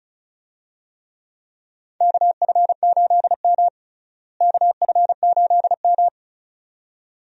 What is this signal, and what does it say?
Morse code, heard as a keyed tone, about 34 wpm, 700 Hz. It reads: KF8M KF8M
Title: USA Call Signs - 4 Characters - Repeated 2x - 35wpm